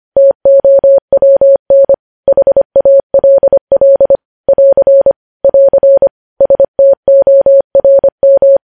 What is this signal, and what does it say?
Morse code, heard as a keyed tone, about 25 wpm, 550 Hz. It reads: TOWN HALL <AR> <AR> STORM
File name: podcast / News headlines-1x-25x25